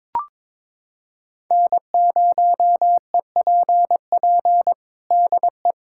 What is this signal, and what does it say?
Morse code, heard as a keyed tone, about 21 wpm, 700 Hz. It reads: N0EPP DE
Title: USA Callsign DE Callsign - 5 Characters - 22wpm